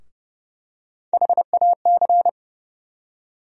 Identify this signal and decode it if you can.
Morse code, heard as a keyed tone, about 30 wpm, 700 Hz. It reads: HAC
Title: USA Call Sign Postfixes - 3 Letters 30wpm